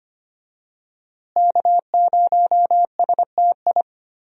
Morse code, heard as a keyed tone, about 25 wpm, 700 Hz. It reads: K0STI
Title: USA Call Signs - 5 Characters - Repeated 2x - 25wpm